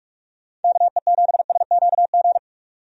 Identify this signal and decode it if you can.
Morse code, heard as a keyed tone, about 43 wpm, 700 Hz. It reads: KE7SQG
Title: USA Call Signs - 6 Characters - 45wpm